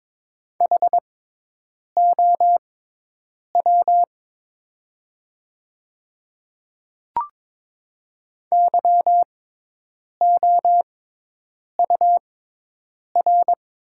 Morse code, heard as a keyed tone, about 22 wpm, spaced as for 8 wpm, 690 Hz. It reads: HOW YOUR
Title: 6x Character Spacing - Top 100 Words - 22wpm